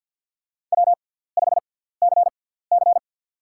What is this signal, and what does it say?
Morse code, heard as a keyed tone, about 52 wpm, spaced as for 17 wpm, 690 Hz. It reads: W5CC
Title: USA Call Signs - 4 Characters - 6x Character Spacing - 50wpm